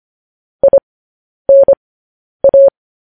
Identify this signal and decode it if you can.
Morse code, heard as a keyed tone, about 25 wpm, spaced as for 10 wpm, 550 Hz. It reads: INA
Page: news-headlines-in-morse-code-at-10-wpm